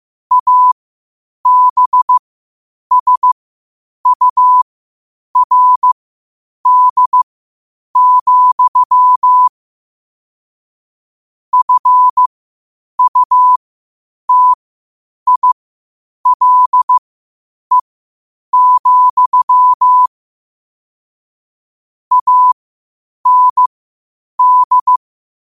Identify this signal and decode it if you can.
Morse code, heard as a keyed tone, about 15 wpm, spaced as for 8 wpm, 1000 Hz. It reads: ABSURD, FUTILE, AND
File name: qotdfiles / 8wpm